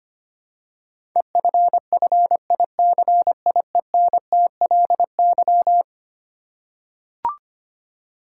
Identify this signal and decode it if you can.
Morse code, heard as a keyed tone, about 25 wpm, 700 Hz. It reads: EFFICIENTLY
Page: morse-code-ninja-practice-25wpm